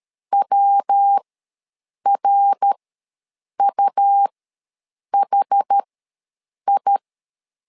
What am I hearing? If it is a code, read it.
Morse code, heard as a keyed tone, about 13 wpm, spaced as for 7 wpm, 800 Hz. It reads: WRUHI
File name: LEZIONI / 3 MWRO / 3 MWR ws 12wpm 1809